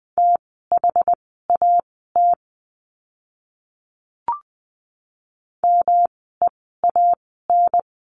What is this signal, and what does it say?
Morse code, heard as a keyed tone, about 20 wpm, spaced as for 15 wpm, 700 Hz. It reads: THAT MEAN
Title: 108 Lesson 18f - Letter m - Words 20wpm